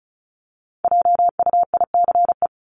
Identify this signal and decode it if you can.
Morse code, heard as a keyed tone, about 34 wpm, 700 Hz. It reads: JUICE
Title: Top 1251-1500 Words 35wpm